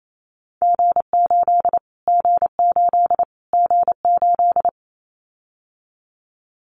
Morse code, heard as a keyed tone, about 28 wpm, 700 Hz. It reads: G8 G8 G8
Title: Callsign Prefixes for Top 25 DXCC - Repeated 3x - 28wpm